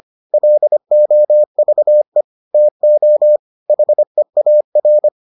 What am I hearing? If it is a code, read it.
Morse code, heard as a keyed tone, about 25 wpm, 600 Hz. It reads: LOVE TO HEAR